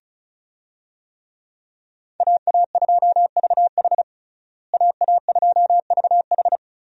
Morse code, heard as a keyed tone, about 36 wpm, 700 Hz. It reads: AA2VH AA2VH
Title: USA Call Signs - 5 Characters - Repeated 3x - 35wpm